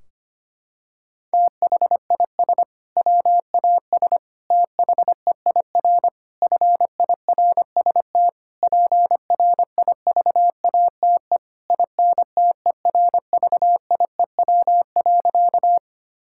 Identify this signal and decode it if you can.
Morse code, heard as a keyed tone, about 25 wpm, 700 Hz. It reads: THIS WAS THEIR FIRST PRIVATE INTERVIEW.
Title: Sentences from Top 1600 Words 25wpm